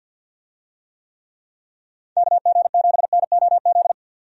Morse code, heard as a keyed tone, about 46 wpm, 700 Hz. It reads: KG7NOZ
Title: USA Call Signs - 6 Characters - 50wpm